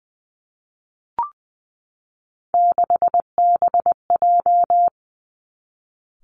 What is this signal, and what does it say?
Morse code, heard as a keyed tone, about 20 wpm, 700 Hz. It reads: 6BJ
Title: Three Challenging Letter-Number Combinations 20wpm